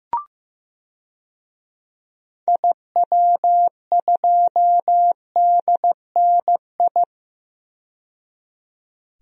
Morse code, heard as a keyed tone, about 15 wpm, 700 Hz. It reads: IW2DNI
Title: DXCC-100 Call Signs - 6 Characters - 15wpm